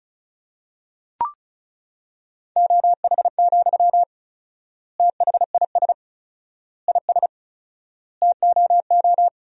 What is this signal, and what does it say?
Morse code, heard as a keyed tone, about 36 wpm, 700 Hz. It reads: OH, THIS IS TOO
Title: 4x Word-Spacing - Sentences from Top 500 Words - 35wpm